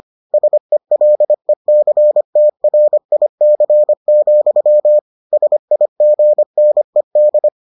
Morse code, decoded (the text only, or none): SELECTRIC, SIGNED